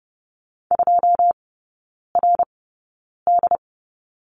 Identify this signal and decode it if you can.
Morse code, heard as a keyed tone, about 30 wpm, spaced as for 9 wpm, 700 Hz. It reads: 2RD
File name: 081 Lesson 14e - Number 2 - Three Characters 30wpm